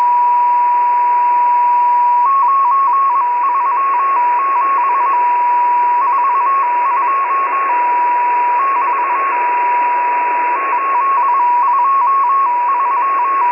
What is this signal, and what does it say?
Morse code, heard as a keyed tone, about 21 wpm, 1100 Hz, with radio noise. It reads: 9 3 7 4 2 <BT> 8 2 3